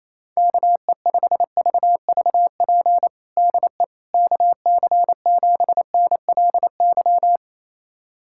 Morse code, heard as a keyed tone, about 28 wpm, 700 Hz. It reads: KE5VVP DE KC7NLY
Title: USA Callsign DE Callsign 28wpm